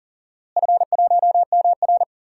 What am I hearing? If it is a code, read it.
Morse code, heard as a keyed tone, about 40 wpm, 700 Hz. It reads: F1MR
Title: DXCC-100 Call Signs - 4 Characters - Repeated 2x - 40wpm